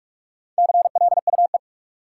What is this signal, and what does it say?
Morse code, heard as a keyed tone, about 46 wpm, 700 Hz. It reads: CLUE
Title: Four Letter Words - 45wpm